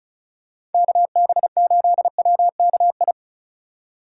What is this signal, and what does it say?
Morse code, heard as a keyed tone, about 34 wpm, 700 Hz. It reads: KB8WKI
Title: USA Call Signs - 6 Characters - 3x Character Spacing - 35wpm